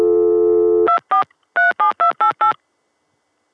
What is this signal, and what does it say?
DTMF, heard as telephone touch tones, tones of about 120 ms. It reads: 2437288